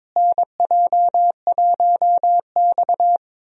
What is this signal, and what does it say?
Morse code, heard as a keyed tone, about 22 wpm, 700 Hz. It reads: NJ1X
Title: USA Call Signs - 4 Characters - Repeated 3x - 22wpm